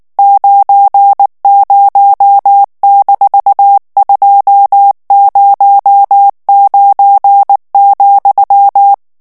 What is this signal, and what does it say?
Morse code, heard as a keyed tone, about 19 wpm, 800 Hz. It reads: 90-209,
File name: ANGLAIS texte clair 3 vitesse 18 mots minute